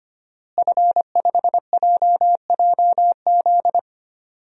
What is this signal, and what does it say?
Morse code, heard as a keyed tone, about 25 wpm, 700 Hz. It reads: F5JJZ
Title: DXCC-100 Call Signs - 5 Characters - Repeated 3x - 25wpm